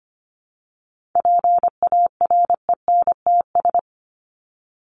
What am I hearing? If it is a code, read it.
Morse code, heard as a keyed tone, about 25 wpm, 700 Hz. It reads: PARENTS
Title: ES:S Suffix - Top 100 Words with Base Words 25wpm